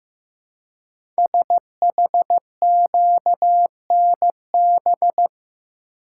Morse code, heard as a keyed tone, about 15 wpm, 700 Hz. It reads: SHQNB
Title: Code Groups - 5 Letters - 15wpm